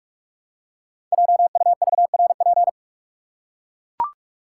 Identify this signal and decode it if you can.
Morse code, heard as a keyed tone, about 44 wpm, 700 Hz. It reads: JUVRP
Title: Code Groups - 5 Letters - 45wpm